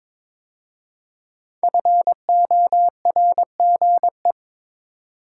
Morse code, heard as a keyed tone, about 23 wpm, 700 Hz. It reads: FORGE
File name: 5 Letter Words - First 3 Letters Given - 22wpm